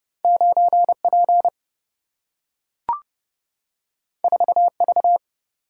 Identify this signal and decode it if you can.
Morse code, heard as a keyed tone, about 30 wpm, 700 Hz. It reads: 9P 4V
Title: Single-Letter Single-Number 30wpm